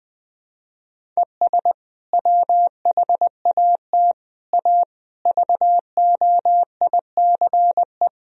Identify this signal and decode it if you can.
Morse code, heard as a keyed tone, about 20 wpm, 700 Hz. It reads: ES WHAT A VOICE
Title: Sentences from Top 500 Words and Abbreviations 20wpm